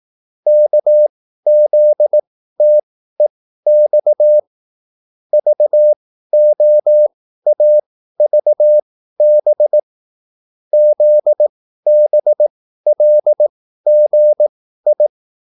Morse code, heard as a keyed tone, about 18 wpm, spaced as for 13 wpm, 600 Hz. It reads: KZTEX VOAVB ZBLGI
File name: QSL / CW lesson